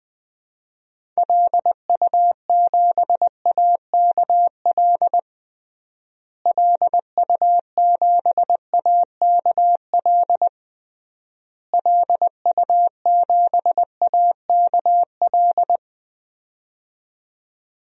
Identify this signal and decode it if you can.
Morse code, heard as a keyed tone, about 20 wpm, 700 Hz. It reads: LU7AKL LU7AKL LU7AKL
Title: DXCC-100 Call Signs - 6 Characters - Repeated 3x - 20wpm